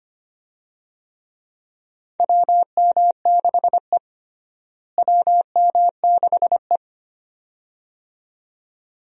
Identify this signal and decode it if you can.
Morse code, heard as a keyed tone, about 25 wpm, 700 Hz. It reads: WM6E WM6E
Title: USA Call Signs - 4 Characters - Repeated 2x - 25wpm